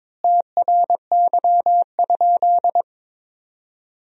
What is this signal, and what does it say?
Morse code, heard as a keyed tone, about 22 wpm, 700 Hz. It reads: TRY?
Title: 4x Word-Spacing - Sentences from Top 200 Words - 22wpm